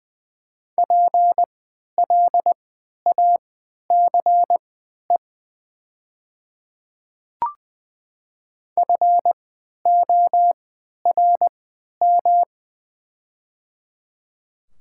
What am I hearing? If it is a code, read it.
Morse code, heard as a keyed tone, about 20 wpm, spaced as for 11 wpm, 700 Hz. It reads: PLACE FORM
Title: 3x Character Spacing - Top 200 Words - 20wpm